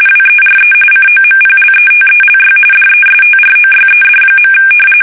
RTTY, about 45.45 baud. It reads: WERE COMPLETELY DESTROYED.
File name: fsk850 / usnavy3